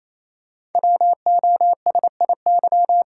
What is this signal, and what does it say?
Morse code, heard as a keyed tone, about 29 wpm, 700 Hz. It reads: WOSIY
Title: Code Groups - 5 Letter:Numbers - 28wpm